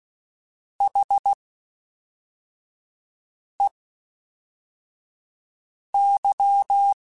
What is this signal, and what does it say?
Morse code, heard as a keyed tone, about 16 wpm, spaced as for 4 wpm, 800 Hz. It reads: HEY
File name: H Prefixes / H mp3s